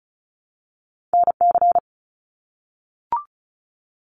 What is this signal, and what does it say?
Morse code, heard as a keyed tone, about 34 wpm, 700 Hz. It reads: NC
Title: US States Repeated 3x 35wpm